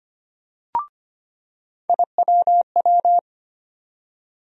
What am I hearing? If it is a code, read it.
Morse code, heard as a keyed tone, about 25 wpm, 700 Hz. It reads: IWW
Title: USA Call Sign Postfixes - 3 Letters 25wpm